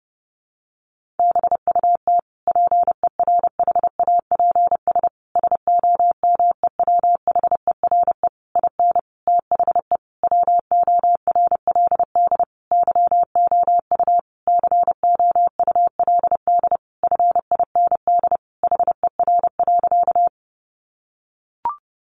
Morse code, heard as a keyed tone, about 28 wpm, 700 Hz. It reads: BUT PERHAPS SOMEWHERE IN THE WORLD YOU COULD FIND HER.
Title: Sentences from Top 1400 Words 30wpm